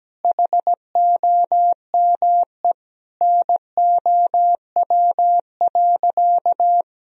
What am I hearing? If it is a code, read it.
Morse code, heard as a keyed tone, about 17 wpm, 700 Hz. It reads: HOME NOW.
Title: Sentences from Top 1000 Words - Review Part 3 - 17wpm